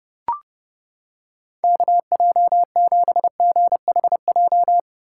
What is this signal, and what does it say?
Morse code, heard as a keyed tone, about 29 wpm, 700 Hz. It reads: KJ7GHJ